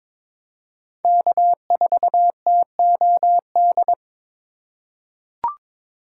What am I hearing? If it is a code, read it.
Morse code, heard as a keyed tone, about 22 wpm, 700 Hz. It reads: K4TOD
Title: USA Call Signs - 5 Characters - 4x Character Spacing - 22wpm